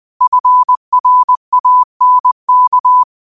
Morse code, heard as a keyed tone, about 20 wpm, 1000 Hz. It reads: FRANK
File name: qotdfiles / 20wpm